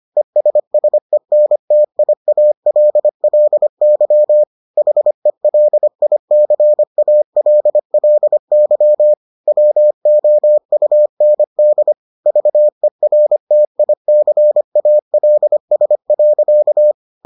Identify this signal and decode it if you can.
Morse code, heard as a keyed tone, about 25 wpm, 600 Hz. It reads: ESSENTIALLY HELICALLY WOUND VERTICALS.